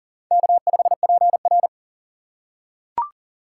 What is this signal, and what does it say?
Morse code, heard as a keyed tone, about 40 wpm, 700 Hz. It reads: K5PR
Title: USA Call Signs - 8x Character Spacing - 40wpm